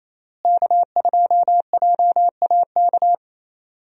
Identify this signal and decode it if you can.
Morse code, heard as a keyed tone, about 28 wpm, 700 Hz. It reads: K2JAK
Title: USA Call Signs - 5 Characters - Repeated 2x - 28wpm